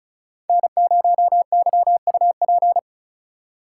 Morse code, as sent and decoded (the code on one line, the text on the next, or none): -. ----- -.-- ..- .--.
N0YUP